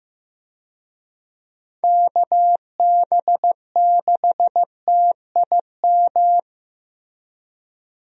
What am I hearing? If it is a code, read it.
Morse code, heard as a keyed tone, about 15 wpm, 700 Hz. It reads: KB6TIM
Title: USA Call Signs - 6 Characters - Repeated 3x - 15wpm